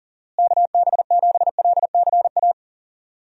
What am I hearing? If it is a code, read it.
Morse code, heard as a keyed tone, about 40 wpm, 700 Hz. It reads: KB7LCA
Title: USA Call Signs - 6 Characters - Repeated 2x - 40wpm